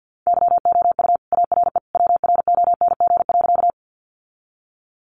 Morse code, heard as a keyed tone, about 50 wpm, 700 Hz. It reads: YOU ARE WRONG.
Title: Sentences from Top 900 Words 50wpm